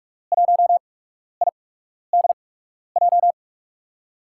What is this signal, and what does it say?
Morse code, heard as a keyed tone, about 43 wpm, spaced as for 12 wpm, 700 Hz. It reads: 1IDJ
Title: USA Call Signs - 5 Characters - 8x Character Spacing - 45wpm